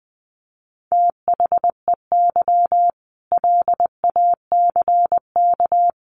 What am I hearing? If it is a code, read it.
Morse code, heard as a keyed tone, about 20 wpm, 700 Hz. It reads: THEY LACK
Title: Sentences from Top 1100 Words 20wpm